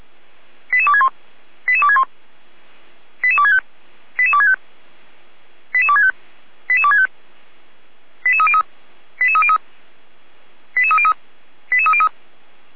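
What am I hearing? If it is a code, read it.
Five-tone selective call (ZVEI), heard as quick runs of five tones, tones of about 70 ms. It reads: 88261 88261 88265 88265 88265 88265 88303 88303 88303 88303
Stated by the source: ZVEI five-tone selective call 88255 received by SDR.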